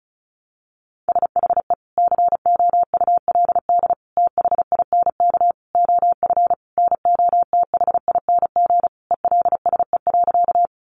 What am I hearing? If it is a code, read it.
Morse code, heard as a keyed tone, about 34 wpm, 700 Hz. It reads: SHE COULD THINK OF NOTHING ELSE.
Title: Sentences from Top 800 Words 35wpm